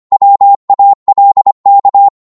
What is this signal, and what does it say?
Morse code, heard as a keyed tone, about 25 wpm, 800 Hz. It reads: WALK